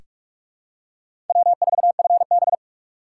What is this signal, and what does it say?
Morse code, heard as a keyed tone, about 46 wpm, 700 Hz. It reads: W4FB
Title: USA Call Signs - 3x Character Spacing - 45wpm